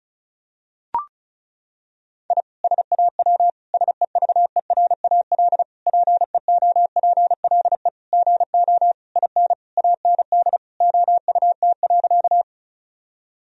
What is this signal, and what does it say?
Morse code, heard as a keyed tone, about 34 wpm, 700 Hz. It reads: I SAW SEVERAL PEOPLE GO IN AND OUT.
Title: Sentences from Top 1000 Words - Review Part 2 - 35wpm